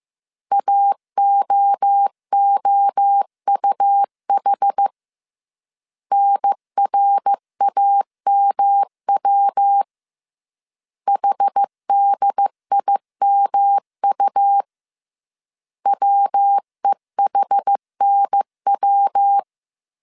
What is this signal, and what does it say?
Morse code, heard as a keyed tone, about 15 wpm, 800 Hz. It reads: AOOUH NRAMW HDIMU WEHNW